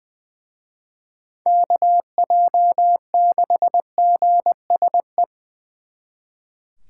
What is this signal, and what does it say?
Morse code, heard as a keyed tone, about 24 wpm, 700 Hz. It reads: KJ6GSE
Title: Speed Racing - USA 2x3 Call Signs 20wpm